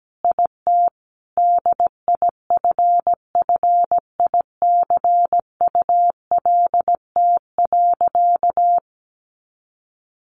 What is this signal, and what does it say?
Morse code, heard as a keyed tone, about 17 wpm, 700 Hz. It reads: IT DIFFICULT.
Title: Sentences from Top 1100 Words 17wpm